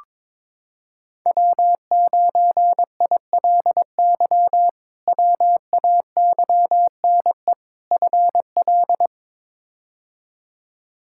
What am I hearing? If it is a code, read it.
Morse code, heard as a keyed tone, about 22 wpm, 700 Hz. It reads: W9ILY WAYNE FL
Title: K1USN SST QSO Practice - 22wpm